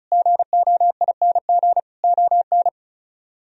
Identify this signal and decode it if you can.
Morse code, heard as a keyed tone, about 44 wpm, 700 Hz. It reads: GOING ON
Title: Speed-Racing - Sets of 4 Words 35wpm